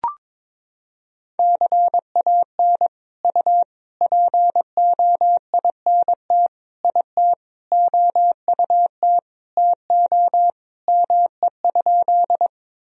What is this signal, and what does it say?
Morse code, heard as a keyed tone, about 22 wpm, 700 Hz. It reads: CAN U POINT IT OUT TO ME?
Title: Sentences from Top 200 Words and Abbreviations 22wpm